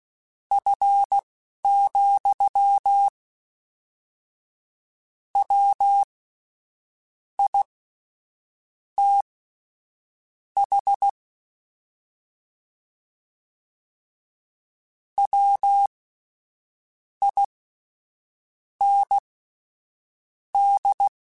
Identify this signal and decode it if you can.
Morse code, heard as a keyed tone, about 16 wpm, spaced as for 6 wpm, 800 Hz. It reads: F, WITH WIND
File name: Spaced Characters / Spaced Characters-mp3s / Weather5 5 sp @10